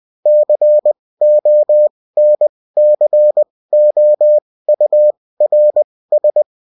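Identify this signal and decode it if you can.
Morse code, heard as a keyed tone, about 20 wpm, spaced as for 16 wpm, 600 Hz. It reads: CONCOURS
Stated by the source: Morse code bulletin of a French amateur CW union, sent as text and in CW.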